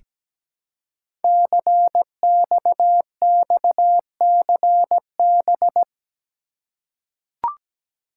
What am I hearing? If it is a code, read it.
Morse code, heard as a keyed tone, about 17 wpm, 700 Hz. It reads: CXXCB
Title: Code Groups - 5 Letters - 17wpm